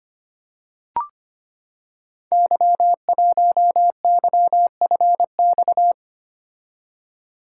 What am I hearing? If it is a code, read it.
Morse code, heard as a keyed tone, about 25 wpm, 700 Hz. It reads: Y1YFX
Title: Code Groups - 5 Letter:Numbers - 25wpm